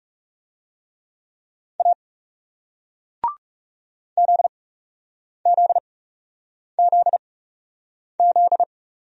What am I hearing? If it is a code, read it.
Morse code, heard as a keyed tone, about 39 wpm, spaced as for 8 wpm, 700 Hz. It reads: A ZZZZ